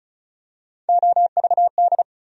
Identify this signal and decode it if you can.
Morse code, heard as a keyed tone, about 36 wpm, 700 Hz. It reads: OVD
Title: USA Call Sign Postfixes - 3 Letters 35wpm